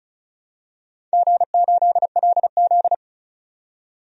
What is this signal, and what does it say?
Morse code, heard as a keyed tone, about 34 wpm, 700 Hz. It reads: G8LZ
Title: DXCC-100 Call Signs - 4 Characters - 35wpm